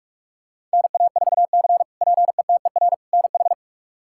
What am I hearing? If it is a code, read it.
Morse code, heard as a keyed tone, about 44 wpm, 700 Hz. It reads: NA4C PETER NH
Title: K1USN SST QSO Practice - 45wpm